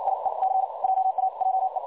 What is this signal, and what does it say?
Morse code, heard as a keyed tone, about 56 wpm, spaced as for 47 wpm, 760 Hz. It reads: IE C JAJE
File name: dlazaawansowanych / nagraniazpasma / websdr recording 3565.0kHz